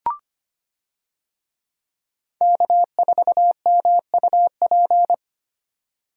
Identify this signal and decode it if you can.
Morse code, heard as a keyed tone, about 25 wpm, 700 Hz. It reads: K4MUP
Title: USA Call Signs - 5 Characters - 25wpm